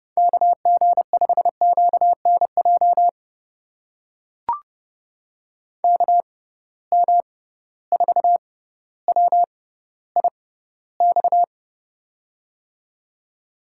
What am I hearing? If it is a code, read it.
Morse code, heard as a keyed tone, about 30 wpm, spaced as for 15 wpm, 700 Hz. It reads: KG5QNJ KM4WIX
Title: USA Call Signs - 6 Characters - 6x Character Spacing - 30wpm